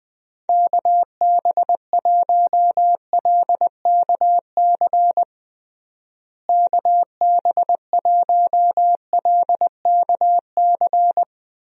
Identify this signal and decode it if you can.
Morse code, heard as a keyed tone, about 20 wpm, 700 Hz. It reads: KB1LKC KB1LKC
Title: USA Call Signs - 6 Characters - Repeated 2x - 20wpm